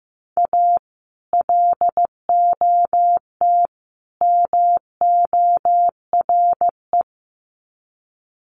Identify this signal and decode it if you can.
Morse code, heard as a keyed tone, about 15 wpm, 700 Hz. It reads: A LOT MORE
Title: Sets of 3 Words 15wpm